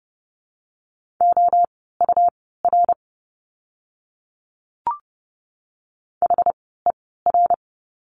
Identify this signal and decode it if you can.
Morse code, heard as a keyed tone, about 30 wpm, spaced as for 17 wpm, 700 Hz. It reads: OUR HER
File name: course / 30:10wpm / 230 Lesson 37f - Letter x - Words 30wpm